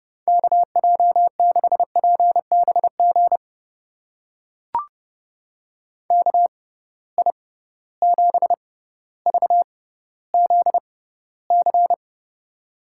Morse code, heard as a keyed tone, about 30 wpm, spaced as for 15 wpm, 700 Hz. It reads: KJ6PBG KI7VZC
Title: USA Call Signs - 6 Characters - 6x Character Spacing - 30wpm